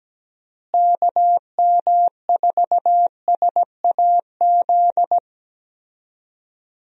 Morse code, heard as a keyed tone, about 17 wpm, 700 Hz. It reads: KM4SAZ
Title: USA Call Signs - 6 Characters - Repeated 2x - 17wpm